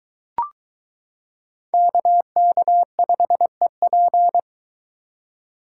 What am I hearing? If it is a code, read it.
Morse code, heard as a keyed tone, about 22 wpm, 700 Hz. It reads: KK5EP